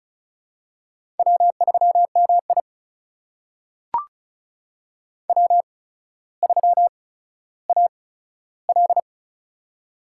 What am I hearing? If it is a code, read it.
Morse code, heard as a keyed tone, about 36 wpm, spaced as for 15 wpm, 700 Hz. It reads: W3MI W3AL
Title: USA Call Signs - 4 Characters - 8x Character Spacing - 35wpm